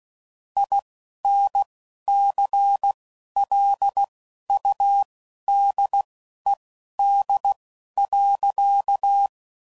Morse code, heard as a keyed tone, about 16 wpm, spaced as for 11 wpm, 800 Hz. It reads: INCLUDED.
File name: Sayings / Quotes / Airtight